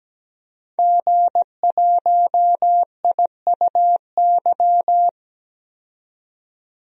Morse code, heard as a keyed tone, about 17 wpm, 700 Hz. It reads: G1IUY
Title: DXCC-100 Call Signs - 5 Characters - 17wpm